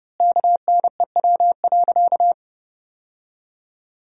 Morse code, heard as a keyed tone, about 30 wpm, 700 Hz. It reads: KNEW.
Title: 3x Word-Spacing - Sentences from Top 500 Words - 30wpm